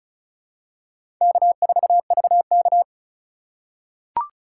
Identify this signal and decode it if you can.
Morse code, heard as a keyed tone, about 36 wpm, 700 Hz. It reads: K4VK
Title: USA Call Signs - 4 Characters - 4x Character Spacing - 35wpm